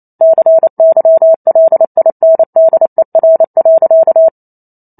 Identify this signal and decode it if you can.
Morse code, heard as a keyed tone, about 28 wpm, 650 Hz. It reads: CYLINDER.
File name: Chapter 2 - 28 WPM